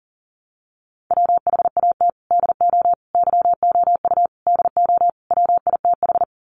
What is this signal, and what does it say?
Morse code, heard as a keyed tone, about 40 wpm, 700 Hz. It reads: WHAT DO YOU DO WITH
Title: Sets of 5 Words Encore2 40wpm